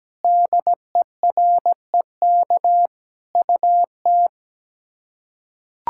Morse code, heard as a keyed tone, about 17 wpm, 700 Hz. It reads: DEREK UT
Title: K1USN SST QSO Practice - 17wpm